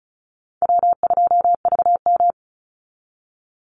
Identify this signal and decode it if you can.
Morse code, heard as a keyed tone, about 34 wpm, 700 Hz. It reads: W2VM
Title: Call Signs - 35wpm